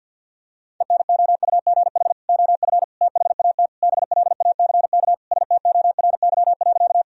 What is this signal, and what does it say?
Morse code, heard as a keyed tone, about 50 wpm, 700 Hz. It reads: ENOUGH OF THAT BLACK STORY.
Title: Sentences from Top 1000 Words - Review Part 2 - 50wpm